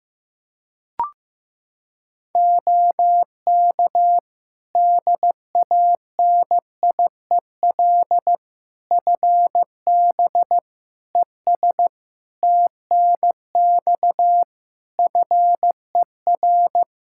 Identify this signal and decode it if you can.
Morse code, heard as a keyed tone, about 16 wpm, 700 Hz. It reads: OK DANIEL FB ES TNX FER
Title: K1USN SST QSO Practice 15wpm